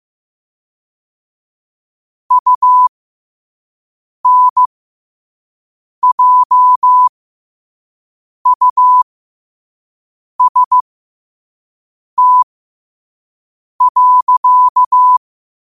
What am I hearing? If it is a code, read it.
Morse code, heard as a keyed tone, about 15 wpm, spaced as for 5 wpm, 1000 Hz. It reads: UNJUST.
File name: qotdfiles / 5wpm